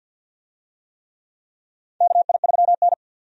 Morse code, heard as a keyed tone, about 52 wpm, 700 Hz. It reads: KI3N
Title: USA Call Signs - 4 Characters - 50wpm